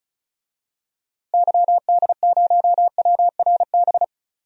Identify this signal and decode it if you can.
Morse code, heard as a keyed tone, about 36 wpm, 700 Hz. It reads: YD0WRB
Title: DXCC-100 Call Signs - 6 Characters - Repeated 3x - 35wpm